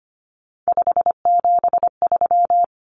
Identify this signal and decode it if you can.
Morse code, heard as a keyed tone, about 25 wpm, 700 Hz. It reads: 573
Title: Numbers - 3-Digit 25wpm